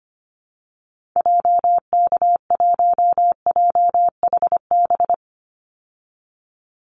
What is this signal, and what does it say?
Morse code, heard as a keyed tone, about 25 wpm, 700 Hz. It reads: JK1JHB
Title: Call Signs Repeated 2x - 25wpm